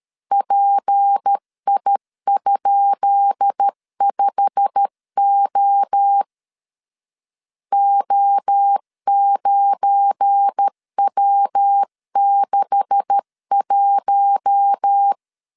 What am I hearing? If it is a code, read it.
Morse code, heard as a keyed tone, about 13 wpm, 800 Hz. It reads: PI?5O O9W61
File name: LEZIONI / 10 SIGNS / 10B INT 12wpm 2414